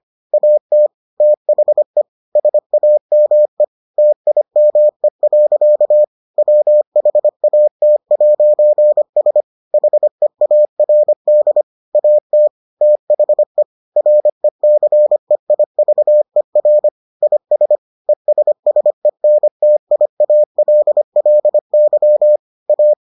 Morse code, heard as a keyed tone, about 25 wpm, 600 Hz. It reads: AT THE SAME TIME. WHAT'S HEARD AT THE RECEIVER IS ESSENTIALLY A